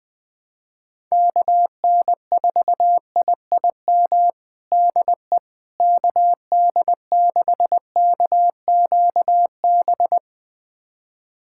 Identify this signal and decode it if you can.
Morse code, heard as a keyed tone, about 20 wpm, 700 Hz. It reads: KN4IIM DE KD6KQB
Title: USA Callsign DE Callsign 20wpm